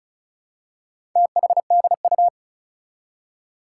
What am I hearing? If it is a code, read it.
Morse code, heard as a keyed tone, about 36 wpm, 700 Hz. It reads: THDU
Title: Four Challenging Letter-Number Combinations 35wpm